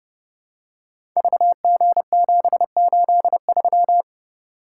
Morse code, heard as a keyed tone, about 30 wpm, 700 Hz. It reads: VG783
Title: Code Groups - 5 Letter:Numbers - 30wpm